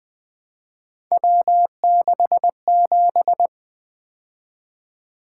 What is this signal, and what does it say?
Morse code, heard as a keyed tone, about 24 wpm, 700 Hz. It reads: W67
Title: Speed-Racing - Three Letter-Number Combinations - 20wpm